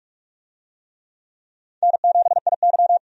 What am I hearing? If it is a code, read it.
Morse code, heard as a keyed tone, about 46 wpm, 700 Hz. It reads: N7IY
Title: USA Call Signs - 4 Characters - 45wpm